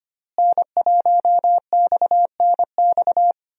Morse code, heard as a keyed tone, about 25 wpm, 700 Hz. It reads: N1XNX
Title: USA Call Signs - 5 Characters - Repeated 3x - 25wpm